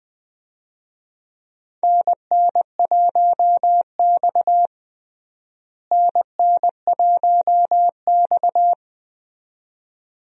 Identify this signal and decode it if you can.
Morse code, heard as a keyed tone, about 21 wpm, 700 Hz. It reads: NN1X NN1X
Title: USA Call Signs - 4 Characters - Repeated 2x - 20wpm